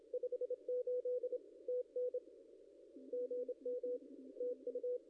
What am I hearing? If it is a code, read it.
Morse code, heard as a keyed tone, about 27 wpm, 500 Hz. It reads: 58 TN GM TU
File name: contests / ss-cw-k5tr-op-n5rz -0700